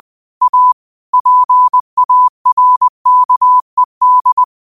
Morse code, heard as a keyed tone, about 20 wpm, 1000 Hz. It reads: A PARKED